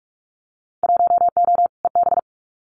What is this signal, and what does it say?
Morse code, heard as a keyed tone, about 44 wpm, 700 Hz. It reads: 1O EB
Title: Winter Field Day Exchanges 45wpm